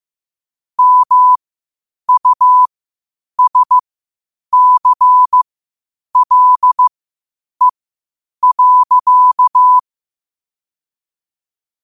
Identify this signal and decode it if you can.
Morse code, heard as a keyed tone, about 15 wpm, spaced as for 8 wpm, 1000 Hz. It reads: MUSCLE.
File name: qotdfiles / 8wpm